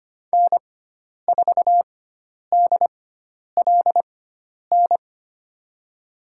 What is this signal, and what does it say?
Morse code, heard as a keyed tone, about 25 wpm, spaced as for 10 wpm, 700 Hz. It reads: N4DLN
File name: Lesson 16g - Letter c - Call Signs 25wpm